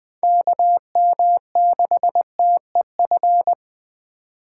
Morse code, heard as a keyed tone, about 20 wpm, 700 Hz. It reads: KM6TEF
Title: USA Call Signs - 6 Characters - 4x Character Spacing - 20wpm